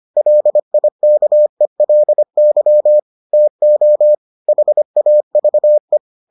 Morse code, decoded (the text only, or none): LIKELY TO HAVE